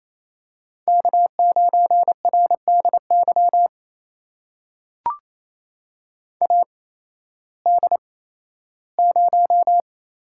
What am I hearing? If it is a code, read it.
Morse code, heard as a keyed tone, about 28 wpm, spaced as for 15 wpm, 700 Hz. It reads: K9RDY AD0
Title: USA Call Signs - 8x Character Spacing - 28wpm